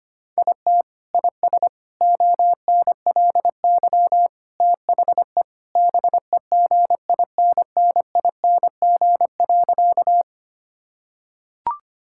Morse code, heard as a keyed tone, about 25 wpm, 700 Hz. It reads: IT IS ONLY THE BEGINNING.
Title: Sentences from Top 1000 Words - Review Part 5 - 25wpm